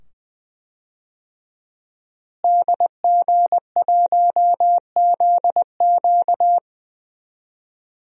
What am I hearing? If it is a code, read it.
Morse code, heard as a keyed tone, about 20 wpm, 700 Hz. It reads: DG1ZQ
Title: DXCC-100 Call Signs - 5 Characters - Repeated 2x - 20wpm